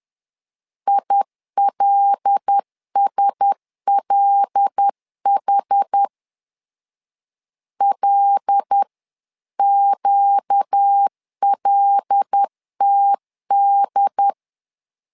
Morse code, heard as a keyed tone, about 10 wpm, 800 Hz. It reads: ILSLH LQLTD
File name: LEZIONI / 5 LBVC / 5 L 10wpm 2259